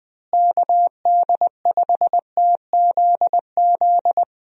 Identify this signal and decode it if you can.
Morse code, heard as a keyed tone, about 20 wpm, 700 Hz. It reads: KD5TZZ
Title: USA Call Signs - 6 Characters - Repeated 2x - 20wpm